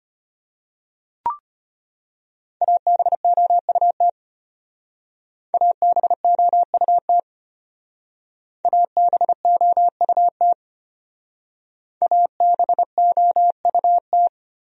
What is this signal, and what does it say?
Morse code, heard as a keyed tone, about 32 wpm, 700 Hz. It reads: ABOUT ABOUT ABOUT ABOUT